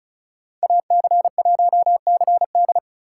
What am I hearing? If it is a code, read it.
Morse code, heard as a keyed tone, about 36 wpm, 700 Hz. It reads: AC1CD
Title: USA Call Signs - 5 Characters - 4x Character Spacing - 35wpm